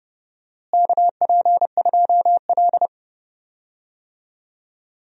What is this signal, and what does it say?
Morse code, heard as a keyed tone, about 30 wpm, 700 Hz. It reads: KP2L
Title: DXCC-100 Call Signs - 4 Characters - Repeated 3x - 30wpm